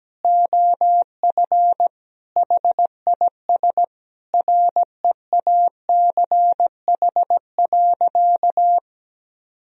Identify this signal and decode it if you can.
Morse code, heard as a keyed tone, about 17 wpm, 700 Hz. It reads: OF HIS REACH.
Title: Sentences from Top 1000 Words - Review Part 2 - 17wpm